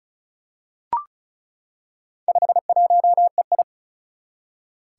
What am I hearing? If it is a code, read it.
Morse code, heard as a keyed tone, about 36 wpm, 700 Hz. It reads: 51EI
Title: Four Challenging Letter-Number Combinations 35wpm